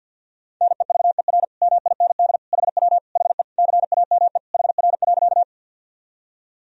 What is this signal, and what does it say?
Morse code, heard as a keyed tone, about 49 wpm, 700 Hz. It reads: NEVER MIND HW HE CAME HR.
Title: Sentences from Top 500 Words and Abbreviations 50wpm